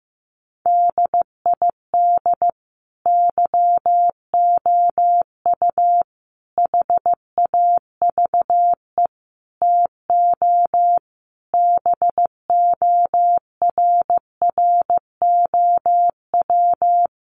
Morse code, heard as a keyed tone, about 15 wpm, 700 Hz. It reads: DID YOU HAVE TO BORROW
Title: Sentences from Top 1500 Words 15wpm